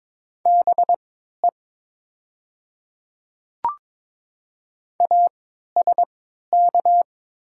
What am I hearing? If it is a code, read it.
Morse code, heard as a keyed tone, about 22 wpm, spaced as for 12 wpm, 700 Hz. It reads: BE ASK
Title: 3x Character Spacing - Top 200 Words - 22wpm